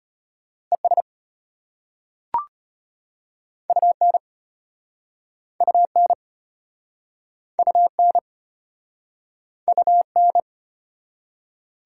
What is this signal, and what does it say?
Morse code, heard as a keyed tone, about 35 wpm, 690 Hz. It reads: ES UN UN UN UN